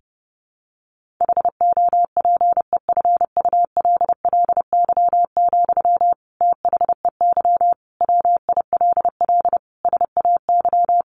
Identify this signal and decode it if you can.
Morse code, heard as a keyed tone, about 30 wpm, 700 Hz. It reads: HOPEFULLY, THEY WILL SAY
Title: Sentences from Top 1500 Words 30wpm